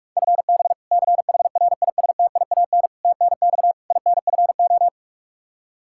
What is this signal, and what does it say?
Morse code, heard as a keyed tone, about 46 wpm, 700 Hz. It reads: FB CHRISTIAN TNX INFO
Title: K1USN SST QSO Practice 45wpm